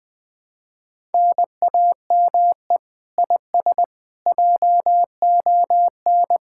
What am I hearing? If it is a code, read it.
Morse code, heard as a keyed tone, about 20 wpm, 700 Hz. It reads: NAME IS JON